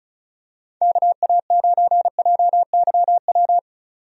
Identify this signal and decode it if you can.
Morse code, heard as a keyed tone, about 35 wpm, 700 Hz. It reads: KA9JYW
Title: USA Call Signs - 6 Characters - 35wpm